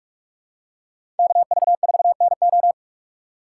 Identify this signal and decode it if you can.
Morse code, heard as a keyed tone, about 45 wpm, 690 Hz. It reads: KV4NO
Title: USA Call Signs - 5 Characters - 6x Character Spacing - 45wpm